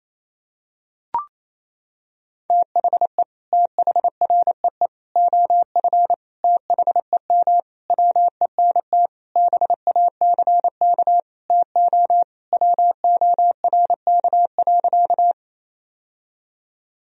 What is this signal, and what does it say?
Morse code, heard as a keyed tone, about 27 wpm, 700 Hz. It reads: THE THREE OF THEM WENT BACK TO WORK.
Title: Sentences from Top 1000 Words - Review Part 1 - 28wpm